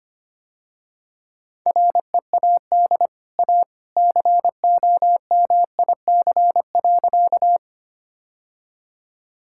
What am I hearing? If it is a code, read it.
Morse code, heard as a keyed tone, about 25 wpm, 700 Hz. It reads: READ A COMIC.
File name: Easy Sentences with Hints - 25wpm